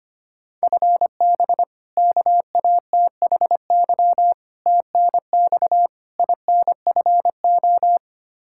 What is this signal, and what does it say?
Morse code, heard as a keyed tone, about 25 wpm, 700 Hz. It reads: FB KATHY TNX INFO
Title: Practice QSO - Exchange 3 - More Info - 25wpm